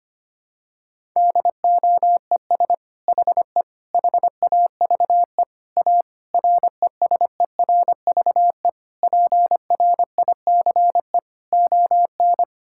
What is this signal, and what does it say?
Morse code, heard as a keyed tone, about 25 wpm, 700 Hz. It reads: DOES HE HAVE A RESERVE PRICE ON
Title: Sentences from Top 1000 Words - Review Part 6 - 25wpm